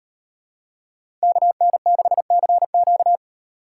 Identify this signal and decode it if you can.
Morse code, heard as a keyed tone, about 38 wpm, 700 Hz. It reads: KN6CQ